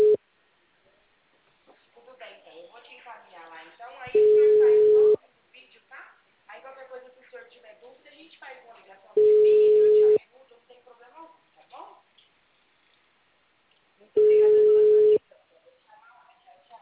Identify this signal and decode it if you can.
Ringback tone (Europe, 425 Hz), with speech.
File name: out- -2005- - - .7832